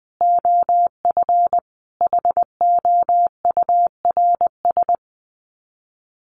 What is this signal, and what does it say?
Morse code, heard as a keyed tone, about 20 wpm, 700 Hz. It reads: OF HOURS
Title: Sets of 5 Words Encore2 20wpm